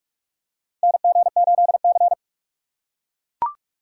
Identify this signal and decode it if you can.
Morse code, heard as a keyed tone, about 42 wpm, 700 Hz. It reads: NG8C
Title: USA Call Signs - 4 Characters - 3x Character Spacing - 45wpm